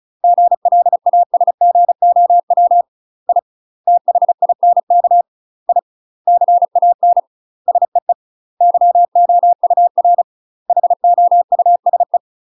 Morse code, heard as a keyed tone, about 34 wpm, 700 Hz. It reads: GLASGOW I THINK I CAN SEE YOUR HOUSE
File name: UK MASTER QSOs TRAINING / 35WPM / UK MASTER QSOs 0003